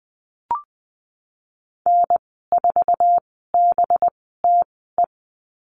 Morse code, heard as a keyed tone, about 20 wpm, spaced as for 15 wpm, 700 Hz. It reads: N4BTE
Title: 186 Lesson 30g - Letter b - Call Signs 20wpm